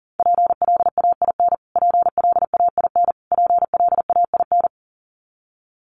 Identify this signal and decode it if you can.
Morse code, heard as a keyed tone, about 40 wpm, 700 Hz. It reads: PLAIN PLAIN PLAIN
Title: Top 500 Words Repeated 40wpm